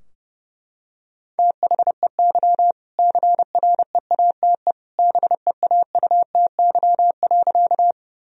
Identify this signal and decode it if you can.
Morse code, heard as a keyed tone, about 30 wpm, 700 Hz. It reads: THEY CREATE BEAUTY.
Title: Sentences from Top 1000 Words - Review Part 3 - 30wpm